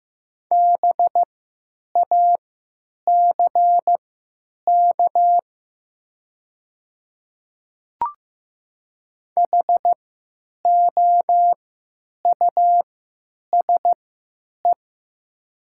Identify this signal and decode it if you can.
Morse code, heard as a keyed tone, about 15 wpm, spaced as for 9 wpm, 700 Hz. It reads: BACK HOUSE
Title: 3x Character Spacing - Top 200 Words - 15wpm